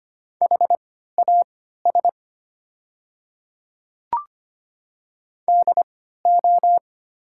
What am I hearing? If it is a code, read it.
Morse code, heard as a keyed tone, about 25 wpm, spaced as for 14 wpm, 700 Hz. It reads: HAS DO
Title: 3x Character Spacing - Top 200 Words - 25wpm